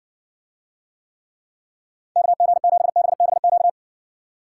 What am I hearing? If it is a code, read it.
Morse code, heard as a keyed tone, about 60 wpm, 700 Hz. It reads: KG7BBQ